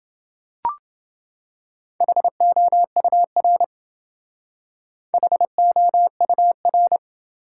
Morse code, heard as a keyed tone, about 28 wpm, 700 Hz. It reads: HOUR HOUR